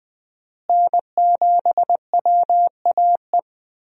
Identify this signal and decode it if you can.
Morse code, heard as a keyed tone, about 20 wpm, 700 Hz. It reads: N7WAE
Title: USA Call Signs - 5 Characters - Repeated 2x - 20wpm